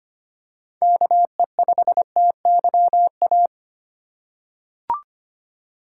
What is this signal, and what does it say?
Morse code, heard as a keyed tone, about 25 wpm, 700 Hz. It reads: KE5TYA
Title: USA Call Signs - 6 Characters - 2x Character Spacing - 25wpm